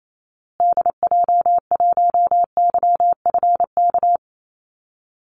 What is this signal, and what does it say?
Morse code, heard as a keyed tone, about 28 wpm, 700 Hz. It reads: DJ1YFK
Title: Call Signs Repeated 2x - 28wpm